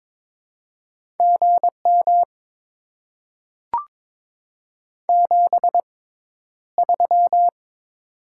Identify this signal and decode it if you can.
Morse code, heard as a keyed tone, about 22 wpm, spaced as for 11 wpm, 690 Hz. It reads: GM 73
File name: Top 20 QSO Elements - 6x Character Spacing - 22wpm